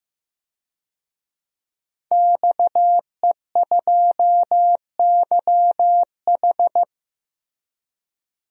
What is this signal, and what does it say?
Morse code, heard as a keyed tone, about 15 wpm, 700 Hz. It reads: XE2YH
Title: DXCC-100 Call Signs - 5 Characters - 15wpm